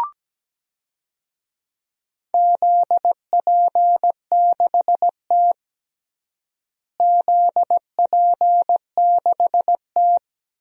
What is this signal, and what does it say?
Morse code, heard as a keyed tone, about 17 wpm, 700 Hz. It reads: ZP6T ZP6T
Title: DXCC-100 Call Signs - 4 Characters - Repeated 3x - 17wpm